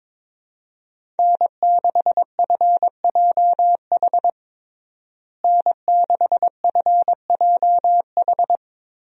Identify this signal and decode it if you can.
Morse code, heard as a keyed tone, about 22 wpm, 700 Hz. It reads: N6FJH N6FJH
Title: USA Call Signs - 5 Characters - Repeated 3x - 22wpm